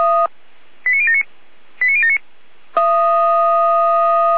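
Five-tone selective call (ZVEI), heard as quick runs of five tones, tones of about 80 ms. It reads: 89079 89079
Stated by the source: ZVEI five-tone selective call 89033 received on SDR.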